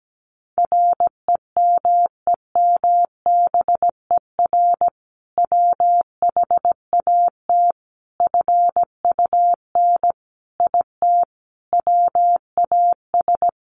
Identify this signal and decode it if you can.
Morse code, heard as a keyed tone, about 17 wpm, 700 Hz. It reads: REMEMBER WHAT FUN IT WAS
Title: Sentences from Top 700 Words 17wpm